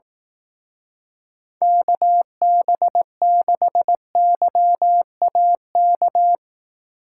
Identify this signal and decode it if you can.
Morse code, heard as a keyed tone, about 18 wpm, 700 Hz. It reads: KB6YAK
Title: Speed Racing - USA Call Signs 15wpm